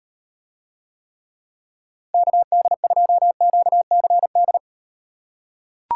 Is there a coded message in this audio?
Yes: Morse code from 2.1 to 4.6 s, heard as a keyed tone, about 38 wpm, 700 Hz. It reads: KD2QCD